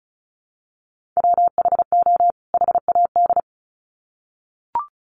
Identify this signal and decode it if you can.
Morse code, heard as a keyed tone, about 34 wpm, 700 Hz. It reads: WHO HAD
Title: Sets of 2 Words 35wpm